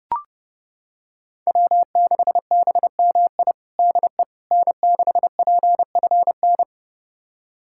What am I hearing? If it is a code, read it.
Morse code, heard as a keyed tone, about 30 wpm, 700 Hz. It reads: W6BMI DE N6PFN
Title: USA Callsign DE Callsign - 5 Characters - 30wpm